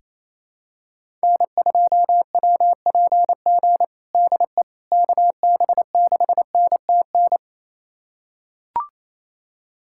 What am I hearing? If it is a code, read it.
Morse code, heard as a keyed tone, about 28 wpm, 700 Hz. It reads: N2WPG DE KB6NTN
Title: USA Callsign DE Callsign 28wpm